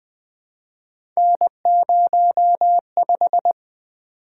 Morse code, heard as a keyed tone, about 20 wpm, 700 Hz. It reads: N05
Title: Three Letter-Number Combinations - 20wpm